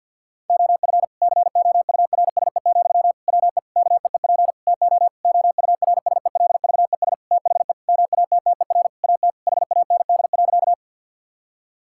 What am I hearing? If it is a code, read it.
Morse code, heard as a keyed tone, about 46 wpm, 700 Hz. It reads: OF COURSE, WE KEEP TO OURSELVES THE MATTER AT HAND.
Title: Sentences from Top 1600 Words 50wpm